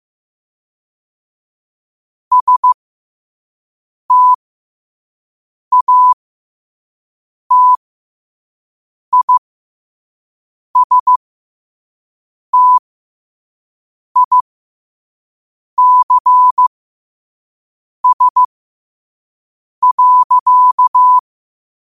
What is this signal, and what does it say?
Morse code, heard as a keyed tone, about 15 wpm, spaced as for 5 wpm, 1000 Hz. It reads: STATISTICS.
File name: qotdfiles / 5wpm